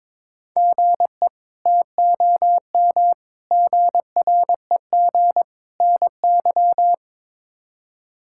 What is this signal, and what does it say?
Morse code, heard as a keyed tone, about 21 wpm, 700 Hz. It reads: GE TOM GREG NY
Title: K1USN SST QSO Practice - 22wpm